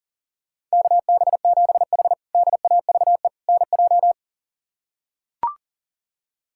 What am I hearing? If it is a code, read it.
Morse code, heard as a keyed tone, about 37 wpm, 700 Hz. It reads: KB7H DAVE NJ
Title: K1USN SST QSO Practice - 40wpm